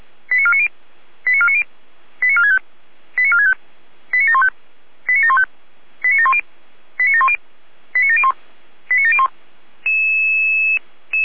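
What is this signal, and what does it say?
Five-tone selective call (ZVEI), heard as quick runs of five tones, tones of about 70 ms. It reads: 89490 89490 89465 89465 89715 89715 89710 89710 89701 89701
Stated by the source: ZVEI five-tone selective call received on an SDR.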